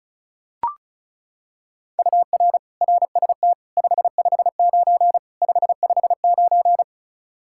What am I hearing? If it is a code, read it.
Morse code, heard as a keyed tone, about 36 wpm, 700 Hz. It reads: UR RST 559 559
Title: Practice QSO - Exchange 1 - Beginning - 35wpm